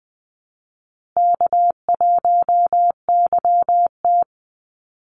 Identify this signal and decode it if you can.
Morse code, heard as a keyed tone, about 20 wpm, 700 Hz. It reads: K1YT
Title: Easy Call Signs Repeated 3x - 20wpm